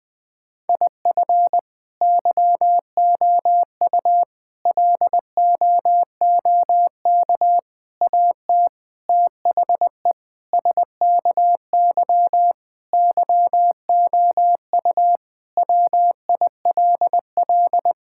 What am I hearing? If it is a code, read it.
Morse code, heard as a keyed tone, about 20 wpm, 700 Hz. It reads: IF YOU LOOK AT THE SKY YOU WILL
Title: Sentences with 4-Word Phrases - Spoken Beginning - 20wpm